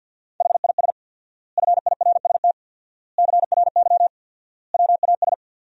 Morse code, heard as a keyed tone, about 47 wpm, 700 Hz. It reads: HIS FIRST CRY WAS
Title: 4x Word-Spacing - Sentences from Top 500 Words - 50wpm